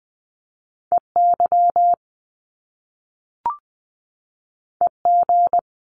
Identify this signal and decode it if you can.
Morse code, heard as a keyed tone, about 20 wpm, 700 Hz. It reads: EY EG
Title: Top Two Letter Combinations in Words 20wpm